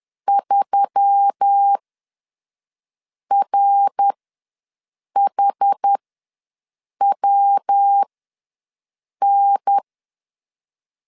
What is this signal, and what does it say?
Morse code, heard as a keyed tone, about 10 wpm, spaced as for 5 wpm, 800 Hz. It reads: 3RHWN
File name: LEZIONI / 9 2867 ws 10wpm 4215